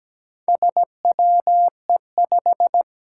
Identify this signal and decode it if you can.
Morse code, heard as a keyed tone, about 17 wpm, 700 Hz. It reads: SWE5
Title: Four Challenging Letter-Number Combinations 17wpm